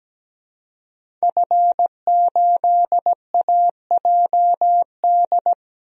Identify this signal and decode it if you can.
Morse code, heard as a keyed tone, about 17 wpm, 700 Hz. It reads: F8AJD
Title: DXCC-100 Call Signs - 5 Characters - 17wpm